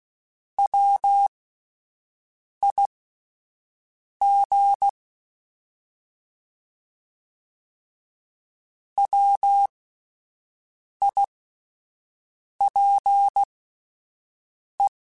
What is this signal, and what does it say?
Morse code, heard as a keyed tone, about 16 wpm, spaced as for 5 wpm, 800 Hz. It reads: WIG WIPE
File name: Alphabetical / Alphabetical mp3s / Ws @5